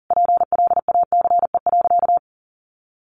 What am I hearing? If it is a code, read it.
Morse code, heard as a keyed tone, about 40 wpm, 700 Hz. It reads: PLACE.
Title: Sentences from Top 1200 Words 40wpm